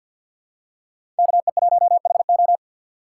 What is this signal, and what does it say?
Morse code, heard as a keyed tone, about 53 wpm, 700 Hz. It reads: KE1HO
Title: USA Call Signs - 5 Characters - 4x Character Spacing - 50wpm